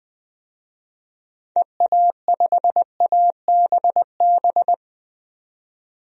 Morse code, heard as a keyed tone, about 20 wpm, 700 Hz. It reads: EA5ABB
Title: DXCC-100 Call Signs - 6 Characters - 20wpm